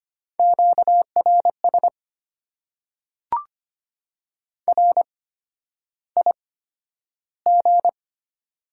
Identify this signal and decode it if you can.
Morse code, heard as a keyed tone, about 25 wpm, spaced as for 11 wpm, 690 Hz. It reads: QRS RIG
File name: Top 90 QSO Elements - 8x Character Spacing - 25wpm